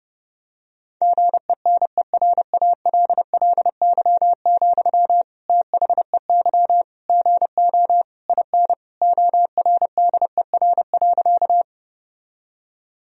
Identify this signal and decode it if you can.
Morse code, heard as a keyed tone, about 30 wpm, 700 Hz. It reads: GENERALLY, THEY GO IN ORDER.
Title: Sentences from Top 1000 Words - Review Part 4 - 30wpm